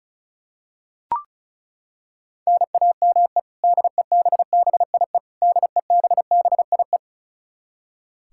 Morse code, heard as a keyed tone, about 36 wpm, 700 Hz. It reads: NAME DEBBIE DEBBIE
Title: Practice QSO - Exchange 1 - Beginning - 35wpm